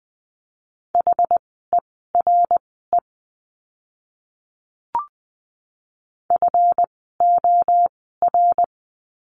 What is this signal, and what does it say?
Morse code, heard as a keyed tone, about 20 wpm, spaced as for 15 wpm, 700 Hz. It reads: HERE FOR
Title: 139 Lesson 23f - Letter f - Words 20wpm